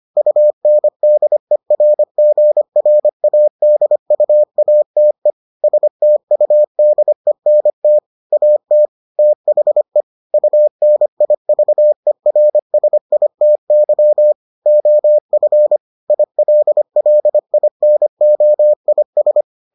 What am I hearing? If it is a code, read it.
Morse code, heard as a keyed tone, about 25 wpm, 600 Hz. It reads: UNDERGRADUATE STUDENT AT THE UNIVERSITY OF ILLINOIS